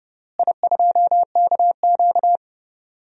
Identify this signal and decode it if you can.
Morse code, heard as a keyed tone, about 30 wpm, 700 Hz. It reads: I2KQ
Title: DXCC-100 Call Signs - 4 Characters - Repeated 3x - 30wpm